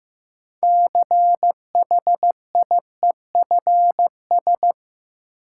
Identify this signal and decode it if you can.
Morse code, heard as a keyed tone, about 15 wpm, 700 Hz. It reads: CHIEFS
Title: Sets of 5 Words Encore - New Word Drill - 15wpm